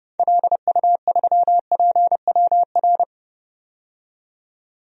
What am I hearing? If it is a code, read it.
Morse code, heard as a keyed tone, about 30 wpm, 700 Hz. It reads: LU3PWR
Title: DXCC-100 Call Signs - 6 Characters - 30wpm